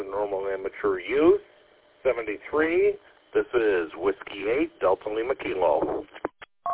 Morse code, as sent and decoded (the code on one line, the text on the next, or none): none